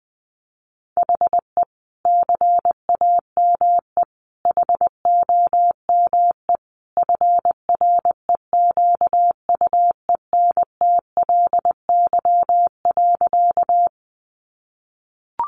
Morse code, heard as a keyed tone, about 20 wpm, 700 Hz. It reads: HE CAME HOME FREQUENTLY.
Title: Sentences from Top 1300 Words 20wpm